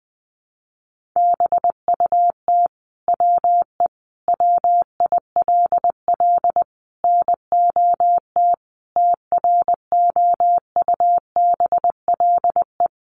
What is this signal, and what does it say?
Morse code, heard as a keyed tone, about 20 wpm, 700 Hz. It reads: BUT WE WILL NOT TROUBLE
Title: Sentences from Top 1200 Words 20wpm